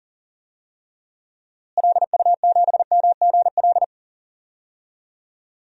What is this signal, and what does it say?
Morse code, heard as a keyed tone, about 40 wpm, 700 Hz. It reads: LU7MGL
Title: DXCC-100 Call Signs - 6 Characters - Repeated 2x - 40wpm